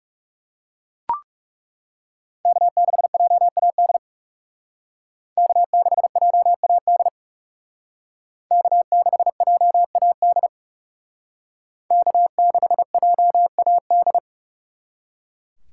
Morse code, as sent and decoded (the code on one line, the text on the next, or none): -.- -.... .--- .- -.. / -.- -.... .--- .- -.. / -.- -.... .--- .- -.. / -.- -.... .--- .- -..
K6JAD K6JAD K6JAD K6JAD